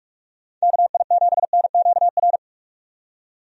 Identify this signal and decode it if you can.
Morse code, heard as a keyed tone, about 43 wpm, 700 Hz. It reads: KI7NQR
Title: USA Call Signs - 6 Characters - 45wpm